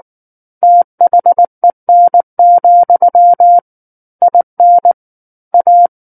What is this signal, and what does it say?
Morse code, heard as a keyed tone, about 19 wpm, 700 Hz. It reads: THEN, IN A